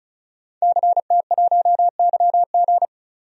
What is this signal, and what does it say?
Morse code, heard as a keyed tone, about 36 wpm, 700 Hz. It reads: CT1YG
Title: DXCC-100 Call Signs - 5 Characters - 35wpm